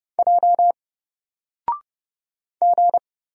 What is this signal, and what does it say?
Morse code, heard as a keyed tone, about 30 wpm, 700 Hz. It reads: J G
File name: Single-Letter-Number-30wpm